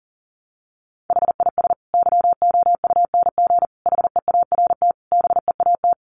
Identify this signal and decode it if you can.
Morse code, heard as a keyed tone, about 40 wpm, 700 Hz. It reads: HIS YOUNG HEART BEAT
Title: Sentences from Top 1300 Words 40wpm